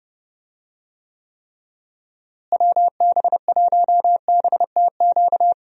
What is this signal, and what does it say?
Morse code, heard as a keyed tone, about 30 wpm, 700 Hz. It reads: WB1BTQ
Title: USA Call Signs - 6 Characters - Repeated 3x - 30wpm